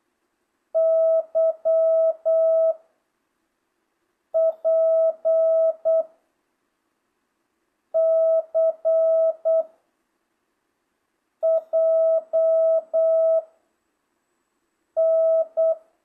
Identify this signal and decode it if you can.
Morse code, heard as a keyed tone, about 8 wpm, 660 Hz. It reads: Y P C J N